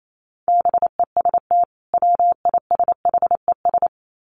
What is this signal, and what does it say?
Morse code, heard as a keyed tone, about 28 wpm, 700 Hz. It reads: BEST WISHES
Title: Common QSO Phrases 28wpm